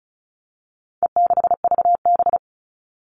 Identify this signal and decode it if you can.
Morse code, heard as a keyed tone, about 34 wpm, 700 Hz. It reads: E6VB
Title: All Cut Numbers - 4-Digit 35wpm